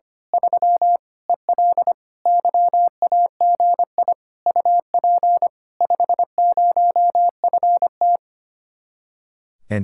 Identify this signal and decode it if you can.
Morse code, heard as a keyed tone, about 24 wpm, 700 Hz. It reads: 3 EL YAGI UP 50FT
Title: Speed-Racing Common QSO Phrases Encore - 25wpm